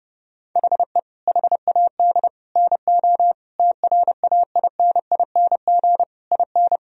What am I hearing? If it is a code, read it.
Morse code, heard as a keyed tone, about 30 wpm, 700 Hz. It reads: HE HAD NO TRAINING IN
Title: Sentences from Top 1000 Words - Review Part 4 - 30wpm